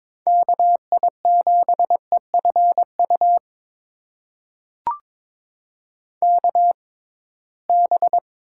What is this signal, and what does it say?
Morse code, heard as a keyed tone, about 22 wpm, spaced as for 17 wpm, 700 Hz. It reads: KI7EFU KB
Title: USA Call Signs - 6 Characters - 6x Character Spacing - 22wpm